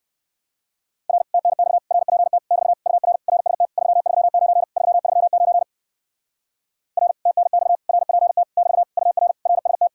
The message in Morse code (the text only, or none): R TNX RPT <BT> UR RST 339 339 R TNX RPT <BT> UR RST